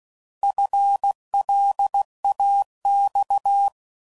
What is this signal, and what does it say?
Morse code, heard as a keyed tone, about 16 wpm, 800 Hz. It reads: FLAX